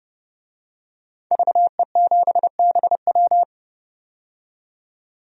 Morse code, heard as a keyed tone, about 32 wpm, 700 Hz. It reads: VE7BW
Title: DXCC-100 Call Signs - 5 Characters - 30wpm